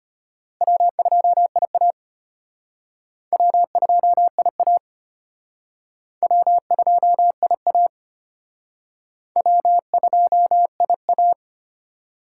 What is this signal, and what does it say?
Morse code, heard as a keyed tone, about 32 wpm, 700 Hz. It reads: W2IA W2IA W2IA W2IA